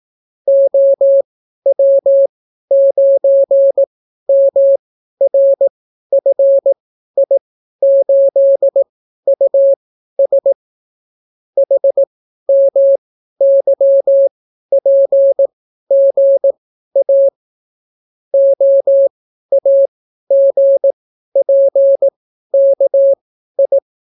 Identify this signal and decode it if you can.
Morse code, heard as a keyed tone, about 18 wpm, spaced as for 12 wpm, 550 Hz. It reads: OW9MRFI8US HMYPGA OAGPKI